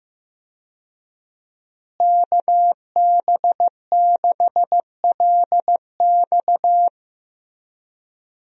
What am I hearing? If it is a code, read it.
Morse code, heard as a keyed tone, about 15 wpm, 700 Hz. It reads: KB6LX
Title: USA Call Signs - 5 Characters - Repeated 2x - 15wpm